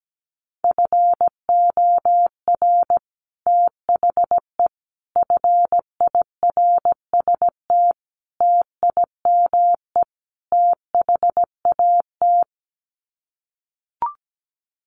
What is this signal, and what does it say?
Morse code, heard as a keyed tone, about 17 wpm, 700 Hz. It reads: FOR THE FIRST TIME THAT
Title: Sets of 5 Words Encore 17wpm